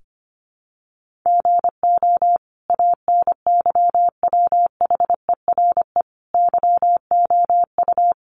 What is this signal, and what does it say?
Morse code, heard as a keyed tone, about 25 wpm, 700 Hz. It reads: GO ANYWHERE YOU
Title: Sentences from Top 1400 Words 25wpm